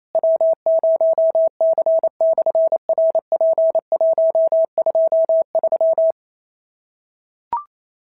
Morse code, heard as a keyed tone, about 27 wpm, 690 Hz. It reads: W0C/RP123
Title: SOTA QSO Practice 28wpm